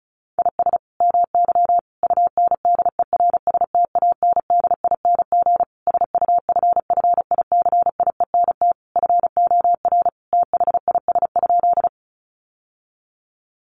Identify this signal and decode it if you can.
Morse code, heard as a keyed tone, about 34 wpm, 700 Hz. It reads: IS MY UNDERSTANDING SUFFICIENT FOR THIS?
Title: Sentences from Top 1400 Words 35wpm